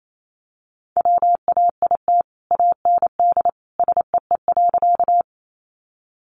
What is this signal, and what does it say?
Morse code, heard as a keyed tone, about 28 wpm, 700 Hz. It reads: WAIT AND SEE.
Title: Sentences from Top 500 Words 28wpm